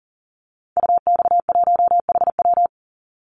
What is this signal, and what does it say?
Morse code, heard as a keyed tone, about 40 wpm, 700 Hz. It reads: UX1HW
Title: Call Signs Repeated 2x - 40wpm